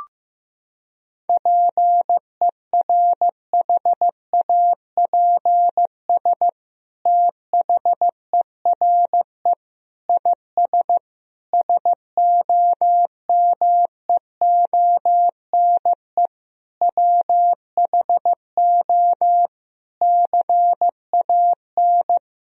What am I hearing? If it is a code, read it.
Morse code, heard as a keyed tone, about 16 wpm, 700 Hz. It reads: PERHAPS THERE IS SOMEONE WHO CAN
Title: Sentences from Top 1000 Words - Review Part 4 - 15wpm